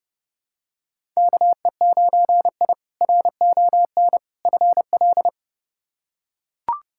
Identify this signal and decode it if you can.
Morse code, heard as a keyed tone, about 32 wpm, 700 Hz. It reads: KE9I RON FL
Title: K1USN SST QSO Practice - 30wpm